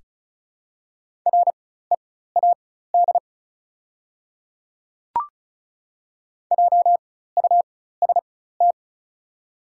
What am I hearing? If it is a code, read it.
Morse code, heard as a keyed tone, about 36 wpm, spaced as for 16 wpm, 700 Hz. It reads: READ JUST
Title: 4x Character Spacing - Top 200 Words - 35wpm